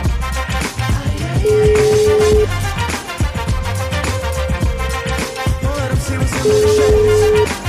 Ringback tone (Europe, 425 Hz), with noise.